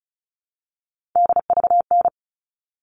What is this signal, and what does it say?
Morse code, heard as a keyed tone, about 34 wpm, 700 Hz. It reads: DVN
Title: All Cut Numbers - 3-Digit 35wpm